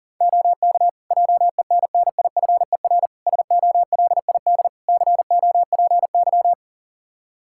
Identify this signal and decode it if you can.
Morse code, heard as a keyed tone, about 40 wpm, 700 Hz. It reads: OK JENNIFER SOLID COPY
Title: Practice QSO - Exchange 3 - More Info - 40wpm